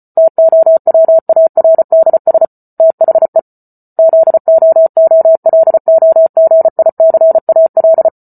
Morse code, heard as a keyed tone, about 34 wpm, 650 Hz. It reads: TOWARDS THE ZOOLOGICAL